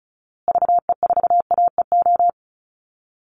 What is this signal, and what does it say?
Morse code, heard as a keyed tone, about 34 wpm, 700 Hz. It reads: VE4AEO
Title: Call Signs Repeated 2x - 35wpm